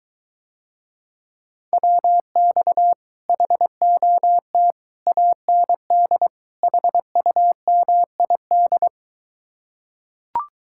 Morse code, heard as a keyed tone, about 23 wpm, 700 Hz. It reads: WX HOT AND HUMID